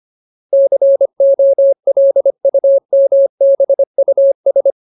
Morse code, heard as a keyed tone, about 25 wpm, 550 Hz. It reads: COLUMBUS
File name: Ultimate Word List / 25WPM Ultimate